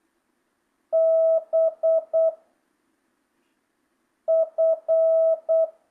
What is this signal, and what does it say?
Morse code, heard as a keyed tone, about 8 wpm, 1300 Hz. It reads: B F
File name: jh8cbh / 2025cw